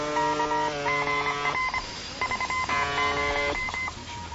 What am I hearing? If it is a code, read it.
Morse code, heard as a keyed tone, about 25 wpm, 2050 Hz, with noise. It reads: GR 30